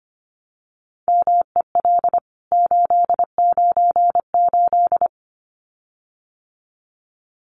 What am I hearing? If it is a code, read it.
Morse code, heard as a keyed tone, about 25 wpm, 700 Hz. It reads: MEL 898
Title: CWT Exchange 25wpm